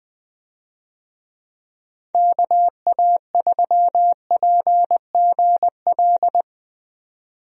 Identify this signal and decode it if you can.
Morse code, heard as a keyed tone, about 20 wpm, 700 Hz. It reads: KA3PGL
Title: USA Call Signs - 6 Characters - 20wpm